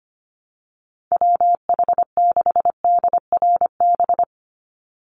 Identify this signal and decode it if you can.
Morse code, heard as a keyed tone, about 25 wpm, 700 Hz. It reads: WH6DRB
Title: USA Call Signs 25wpm